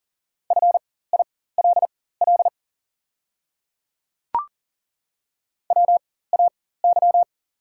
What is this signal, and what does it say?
Morse code, heard as a keyed tone, about 38 wpm, spaced as for 18 wpm, 700 Hz. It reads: FILL WAY
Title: 4x Character Spacing - Top 500 Words - 40wpm